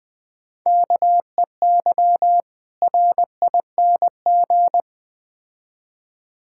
Morse code, heard as a keyed tone, about 21 wpm, 700 Hz. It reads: KEY RING
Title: Difficult Words with Hints - 20wpm